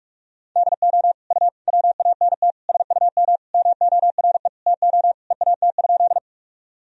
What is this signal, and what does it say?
Morse code, heard as a keyed tone, about 47 wpm, 700 Hz. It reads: DO U WANT SUM MORE TO EAT?
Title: Sentences from Top 300 Words and Abbreviations 45wpm